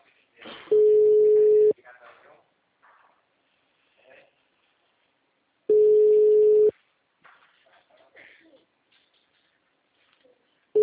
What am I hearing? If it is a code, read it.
Ringback tone (Europe, 425 Hz), with speech.
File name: out- -2014- - - .47